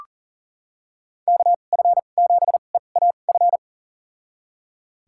Morse code, heard as a keyed tone, about 39 wpm, spaced as for 29 wpm, 700 Hz. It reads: KF7EAF
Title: USA Call Signs - 6 Characters - 2x Character Spacing - 40wpm